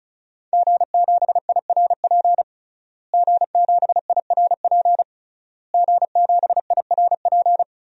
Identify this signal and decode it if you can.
Morse code, heard as a keyed tone, about 36 wpm, 690 Hz. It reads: G7IRP G7IRP G7IRP
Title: DXCC-100 Call Signs - 5 Characters - Repeated 3x - 35wpm